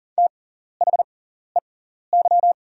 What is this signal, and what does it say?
Morse code, heard as a keyed tone, about 39 wpm, spaced as for 14 wpm, 690 Hz. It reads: THEY
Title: 6x Character Spacing - Top 100 Words - 40wpm